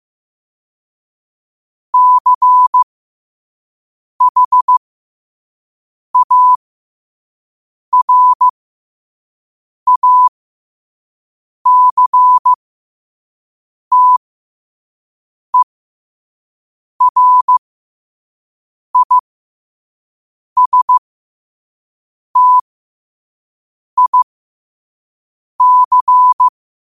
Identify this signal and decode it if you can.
Morse code, heard as a keyed tone, about 15 wpm, spaced as for 5 wpm, 1000 Hz. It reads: CHARACTERISTIC
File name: qotdfiles / 5wpm